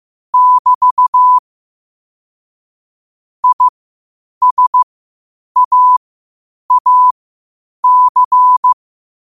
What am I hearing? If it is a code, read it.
Morse code, heard as a keyed tone, about 15 wpm, spaced as for 8 wpm, 1000 Hz. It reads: <BT> ISAAC
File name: qotdfiles / 8wpm